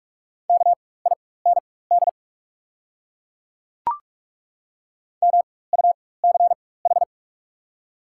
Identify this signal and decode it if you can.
Morse code, heard as a keyed tone, about 42 wpm, spaced as for 21 wpm, 690 Hz. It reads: KIND MUCH
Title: 4x Character Spacing - Top 200 Words - 45wpm